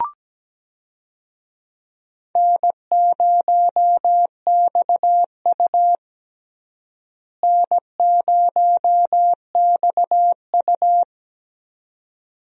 Morse code, heard as a keyed tone, about 17 wpm, 700 Hz. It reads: N0XU N0XU
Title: USA Call Signs - 4 Characters - Repeated 2x - 17wpm